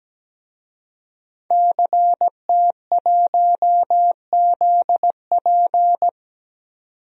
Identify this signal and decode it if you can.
Morse code, heard as a keyed tone, about 17 wpm, 700 Hz. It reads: CT1ZP
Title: DXCC-100 Call Signs - 5 Characters - 17wpm